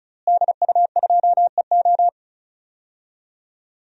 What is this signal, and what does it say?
Morse code, heard as a keyed tone, about 34 wpm, 700 Hz. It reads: DU2EO
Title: DXCC-100 Call Signs - 5 Characters - Repeated 3x - 35wpm